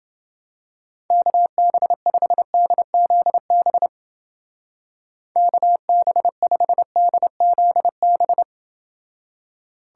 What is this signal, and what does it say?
Morse code, heard as a keyed tone, about 28 wpm, 700 Hz. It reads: KB5DZB KB5DZB